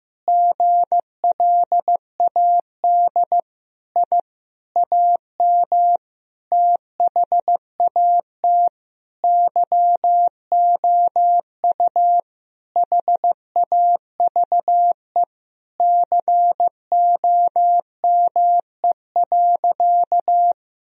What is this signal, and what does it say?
Morse code, heard as a keyed tone, about 15 wpm, 700 Hz. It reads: GLAD I AM THAT YOU HAVE COME.
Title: Sentences from Top 1000 Words - Review Part 2 - 15wpm